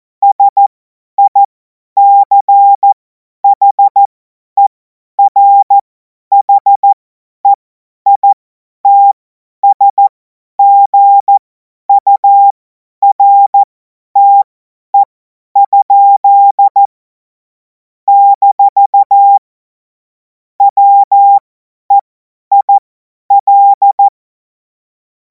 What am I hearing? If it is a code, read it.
Morse code, heard as a keyed tone, about 14 wpm, spaced as for 10 wpm, 800 Hz. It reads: SICHERHEITSGURTE? - WEIL
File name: Morsekurs / Witze / Witz 16. Juni 2026 50 F70